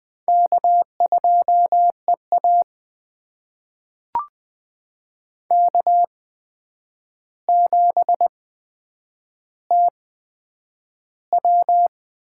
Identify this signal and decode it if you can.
Morse code, heard as a keyed tone, about 20 wpm, spaced as for 9 wpm, 700 Hz. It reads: K2EA K7TW
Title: USA Call Signs - 4 Characters - 8x Character Spacing - 20wpm